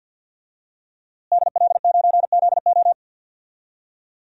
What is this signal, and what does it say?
Morse code, heard as a keyed tone, about 51 wpm, 700 Hz. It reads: DL9ZO
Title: DXCC-100 Call Signs - 5 Characters - Repeated 2x - 50wpm